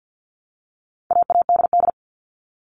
Morse code, heard as a keyed tone, about 50 wpm, 700 Hz. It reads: AADD
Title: All Cut Numbers - 4-Digit 50wpm